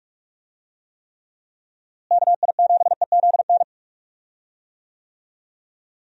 Morse code, heard as a keyed tone, about 46 wpm, 700 Hz. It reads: KI7EZN
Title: USA Call Signs - 6 Characters - Repeated 2x - 45wpm